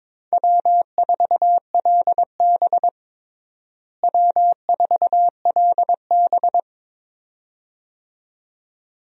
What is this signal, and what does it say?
Morse code, heard as a keyed tone, about 22 wpm, 700 Hz. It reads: W4LB W4LB
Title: USA Call Signs - 4 Characters - Repeated 2x - 22wpm